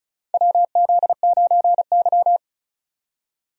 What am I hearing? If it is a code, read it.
Morse code, heard as a keyed tone, about 36 wpm, 700 Hz. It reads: WZ9Y
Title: USA Call Signs - 4 Characters - 35wpm